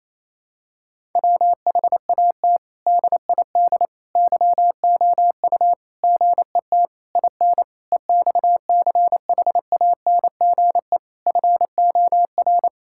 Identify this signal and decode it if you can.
Morse code, heard as a keyed tone, about 28 wpm, 700 Hz. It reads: WHAT DID YOU GET IN EXCHANGE FOR
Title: Sentences from Top 1000 Words - Review Part 6 - 28wpm